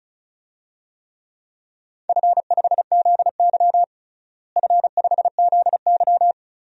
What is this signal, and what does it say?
Morse code, heard as a keyed tone, about 36 wpm, 700 Hz. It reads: F5ZY F5ZY
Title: DXCC-100 Call Signs - 4 Characters - Repeated 2x - 35wpm